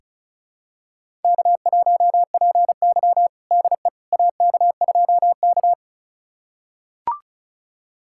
Morse code, heard as a keyed tone, about 34 wpm, 700 Hz. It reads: K1PY DE AK2K
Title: USA Callsign DE Callsign - 4 Characters - 35wpm